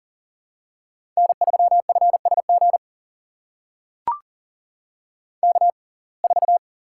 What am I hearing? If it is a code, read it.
Morse code, heard as a keyed tone, about 40 wpm, spaced as for 29 wpm, 700 Hz. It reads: N3FSG K4
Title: USA Call Signs - 5 Characters - 6x Character Spacing - 40wpm